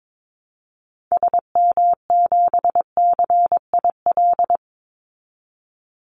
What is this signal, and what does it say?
Morse code, heard as a keyed tone, about 22 wpm, 700 Hz. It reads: SM7CIL
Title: Call Signs Repeated 2x - 22wpm